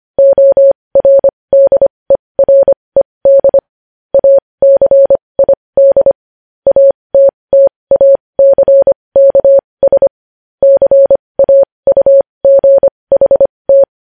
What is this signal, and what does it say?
Morse code, heard as a keyed tone, about 25 wpm, spaced as for 20 wpm, 550 Hz. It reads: ORDERED ACID ATTACKS CAUGHT
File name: podcast / News headlines-1x-25x20